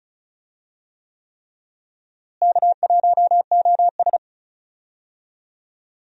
Morse code, heard as a keyed tone, about 36 wpm, 700 Hz. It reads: K1OS
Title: USA Call Signs - 4 Characters - Repeated 3x - 35wpm